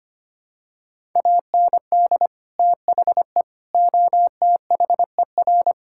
Morse code, heard as a keyed tone, about 25 wpm, 700 Hz. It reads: AND THE OTHER
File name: practice / 25wpm / Sentences with 3-Word Phrases - 25wpm